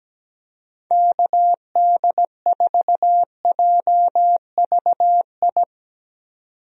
Morse code, heard as a keyed tone, about 17 wpm, 700 Hz. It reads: KD4JVI
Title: USA Call Signs - 6 Characters - 17wpm